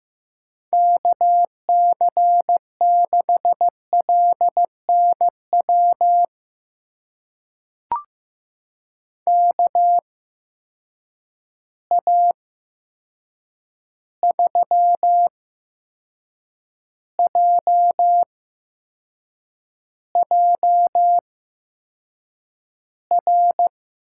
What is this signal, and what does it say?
Morse code, heard as a keyed tone, about 15 wpm, spaced as for 6 wpm, 700 Hz. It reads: KC6LNW KA3JJR
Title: USA Call Signs - 6 Characters - 8x Character Spacing - 15wpm